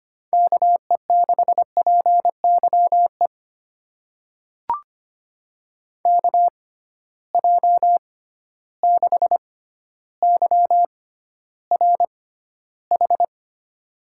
Morse code, heard as a keyed tone, about 25 wpm, spaced as for 13 wpm, 700 Hz. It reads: KE6PYE KJ6YRH
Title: USA Call Signs - 6 Characters - 6x Character Spacing - 25wpm